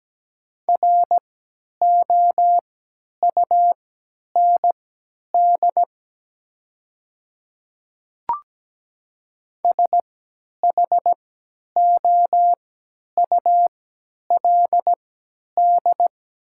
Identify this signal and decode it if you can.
Morse code, heard as a keyed tone, about 17 wpm, spaced as for 10 wpm, 700 Hz. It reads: ROUND SHOULD
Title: 3x Character Spacing - Top 300 Words - 17wpm